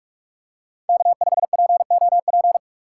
Morse code, heard as a keyed tone, about 46 wpm, 700 Hz. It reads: K5POP
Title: USA Call Signs - 5 Characters - 4x Character Spacing - 45wpm